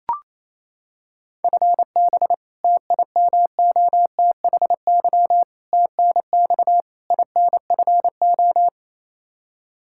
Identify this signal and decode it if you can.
Morse code, heard as a keyed tone, about 27 wpm, 700 Hz. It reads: FB TIMOTHY TNX INFO
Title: K1USN SST QSO Practice 28wpm